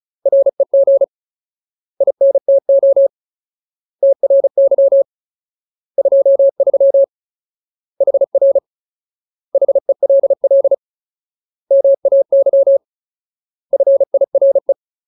Morse code, heard as a keyed tone, about 34 wpm, 550 Hz. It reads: REG INTO TRY 23 HR HELL MAY FIRE
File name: Ultimate Word List / 35WPM Ultimate